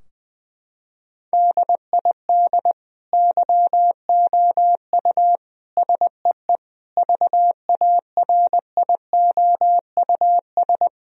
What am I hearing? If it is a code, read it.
Morse code, heard as a keyed tone, about 19 wpm, 700 Hz. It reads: DID YOU SEE VARIOUS
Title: Sentences from Top 1000 Words - Review Part 4 - 20wpm